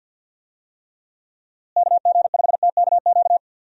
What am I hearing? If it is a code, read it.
Morse code, heard as a keyed tone, about 52 wpm, 700 Hz. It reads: KG5TKQ
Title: USA Call Signs - 6 Characters - Repeated 2x - 50wpm